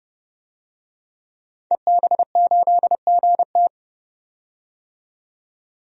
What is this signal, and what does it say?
Morse code, heard as a keyed tone, about 30 wpm, 700 Hz. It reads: EB8GT
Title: DXCC-100 Call Signs - 5 Characters - 30wpm